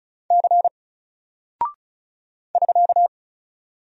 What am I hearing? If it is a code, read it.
Morse code, heard as a keyed tone, about 36 wpm, 700 Hz. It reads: C <SK>
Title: Single Character (Letter:Number:Punctuation:Prosign) Alt 35wpm